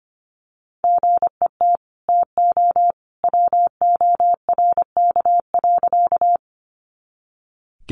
Morse code, heard as a keyed tone, about 25 wpm, 700 Hz. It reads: GET TO WORK.
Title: Sentences from Top 200 Words 25wpm